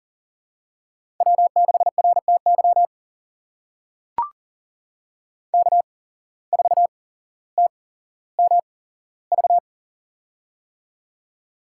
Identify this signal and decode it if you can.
Morse code, heard as a keyed tone, about 40 wpm, spaced as for 17 wpm, 700 Hz. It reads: W6RTY K4TMV
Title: USA Call Signs - 5 Characters - 8x Character Spacing - 40wpm